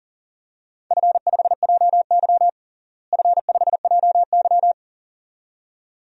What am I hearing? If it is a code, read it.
Morse code, heard as a keyed tone, about 37 wpm, 700 Hz. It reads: F5JY F5JY
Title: DXCC-100 Call Signs - 4 Characters - Repeated 2x - 40wpm